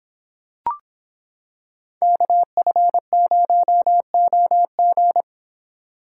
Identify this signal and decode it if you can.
Morse code, heard as a keyed tone, about 25 wpm, 700 Hz. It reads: KF0OG